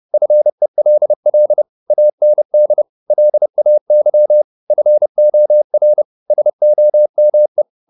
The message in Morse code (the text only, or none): FELL AND LAY FOR SOME